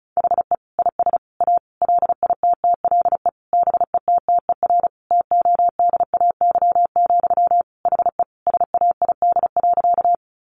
Morse code, heard as a keyed tone, about 34 wpm, 700 Hz. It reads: HE IS A LITTLE BETTER TODAY, HE SAID.
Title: Sentences from Top 600 Words 35wpm